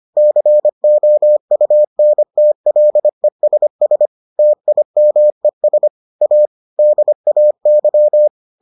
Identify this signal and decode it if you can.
Morse code, heard as a keyed tone, about 25 wpm, 600 Hz. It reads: COUNTLESS TIMES A DAY